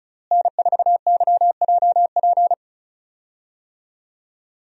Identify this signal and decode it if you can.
Morse code, heard as a keyed tone, about 34 wpm, 700 Hz. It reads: N4YJP
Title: USA Call Signs - 5 Characters - Repeated 3x - 35wpm